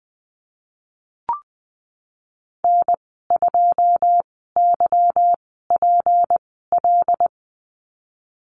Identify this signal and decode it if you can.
Morse code, heard as a keyed tone, about 20 wpm, spaced as for 14 wpm, 700 Hz. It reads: N2YPL
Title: 161 Lesson 26g - Letter g - Call Signs 20wpm